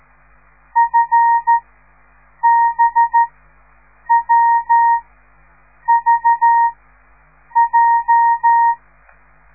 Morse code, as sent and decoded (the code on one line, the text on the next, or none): ..-. -... .-- ...- .---
FBWVJ